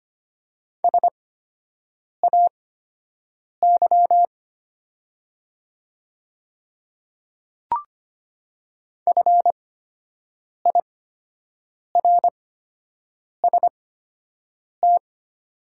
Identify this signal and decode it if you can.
Morse code, heard as a keyed tone, about 25 wpm, spaced as for 7 wpm, 690 Hz. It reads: SAY FIRST
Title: 8x Character Spacing - Top 200 Words - 25wpm